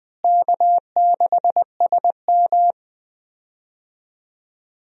Morse code, heard as a keyed tone, about 20 wpm, 700 Hz. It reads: K6SM
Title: USA Call Signs - 4 Characters - Repeated 2x - 20wpm